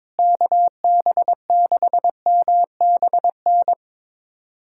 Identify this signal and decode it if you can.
Morse code, heard as a keyed tone, about 22 wpm, 700 Hz. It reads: KB6MBN
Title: USA Callsign DE Callsign 22wpm